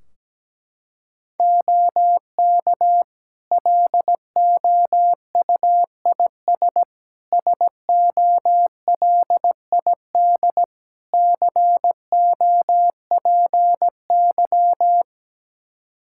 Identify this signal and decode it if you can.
Morse code, heard as a keyed tone, about 17 wpm, 700 Hz. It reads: OK LOUIS SOLID COPY
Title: Practice QSO - Exchange 3 - More Info - 17wpm